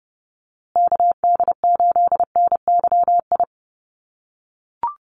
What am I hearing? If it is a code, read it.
Morse code, heard as a keyed tone, about 30 wpm, 700 Hz. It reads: KD8NYI
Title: USA Call Signs Repeated 3x 30wpm